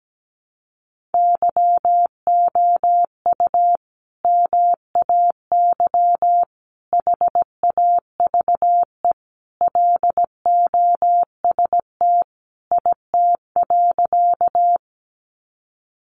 Morse code, heard as a keyed tone, about 17 wpm, 700 Hz. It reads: YOU MAY HAVE LOST IT.
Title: Sentences from Top 1400 Words 17wpm